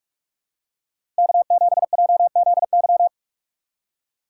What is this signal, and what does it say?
Morse code, heard as a keyed tone, about 46 wpm, 700 Hz. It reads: K7JZY
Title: USA Call Signs - 3x Character Spacing - 45wpm